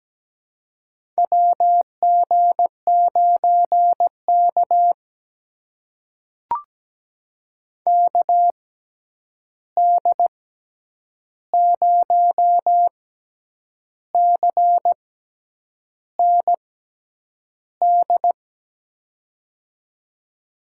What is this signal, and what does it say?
Morse code, heard as a keyed tone, about 17 wpm, spaced as for 8 wpm, 700 Hz. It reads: WG9K KD0CND
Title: USA Call Signs - 6x Character Spacing - 17wpm